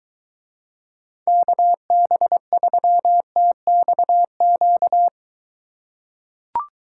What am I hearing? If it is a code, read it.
Morse code, heard as a keyed tone, about 22 wpm, 700 Hz. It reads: KB3TXQ